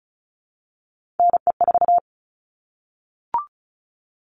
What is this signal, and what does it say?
Morse code, heard as a keyed tone, about 34 wpm, 700 Hz. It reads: NE4
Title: All Cut Numbers - 3-Digit 35wpm